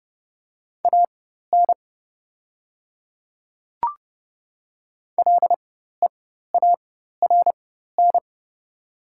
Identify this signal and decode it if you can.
Morse code, heard as a keyed tone, about 30 wpm, spaced as for 14 wpm, 700 Hz. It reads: AN LEARN
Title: 4x Character Spacing - Top 300 Words - 30wpm